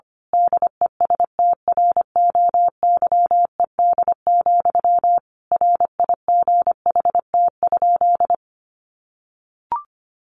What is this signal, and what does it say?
Morse code, heard as a keyed tone, about 25 wpm, 700 Hz. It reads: DESTROYED, RIGHT?
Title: A Sound Of Thunder 25wpm